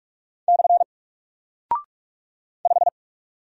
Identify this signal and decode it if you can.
Morse code, heard as a keyed tone, about 44 wpm, 700 Hz. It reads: / 5
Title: Single Character (Letter:Number:Punctuation:Prosign) Alt 45wpm